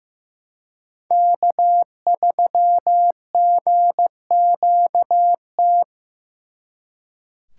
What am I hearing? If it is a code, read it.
Morse code, heard as a keyed tone, about 18 wpm, 700 Hz. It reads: K3GQT
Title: Speed Racing - USA 1x3 Call Signs 15wpm